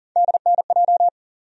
Morse code, heard as a keyed tone, about 40 wpm, 700 Hz. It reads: DNJ
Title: USA Call Sign Postfixes - 3 Letters 40wpm